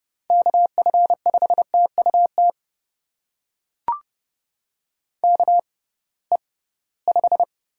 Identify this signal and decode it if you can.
Morse code, heard as a keyed tone, about 30 wpm, spaced as for 19 wpm, 700 Hz. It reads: KF5TUT KE5
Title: USA Call Signs - 6 Characters - 6x Character Spacing - 30wpm